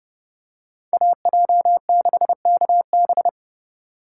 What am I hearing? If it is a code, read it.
Morse code, heard as a keyed tone, about 28 wpm, 700 Hz. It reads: AJ6KB